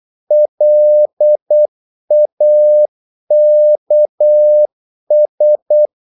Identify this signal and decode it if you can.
Morse code, heard as a keyed tone, about 8 wpm, 600 Hz. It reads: LAKS
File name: Korte ord DK 8wpm